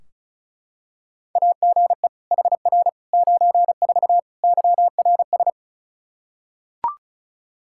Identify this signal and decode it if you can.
Morse code, heard as a keyed tone, about 36 wpm, 700 Hz. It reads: AGE HR 94 YRS
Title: Practice QSO - Exchange 3 - More Info - 35wpm